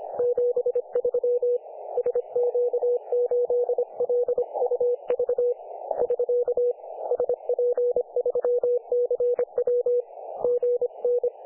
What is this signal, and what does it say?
Morse code, heard as a keyed tone, about 25 wpm, 500 Hz. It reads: 73 SQ8LUV <SK> SP3CW GN
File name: dlazaawansowanych / nagraniazpasma / websdr recording 3564.7kHz